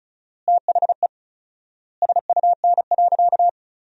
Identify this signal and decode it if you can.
Morse code, heard as a keyed tone, about 36 wpm, 700 Hz. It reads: THE SUN.
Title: 4x Word-Spacing - Sentences from Top 500 Words - 35wpm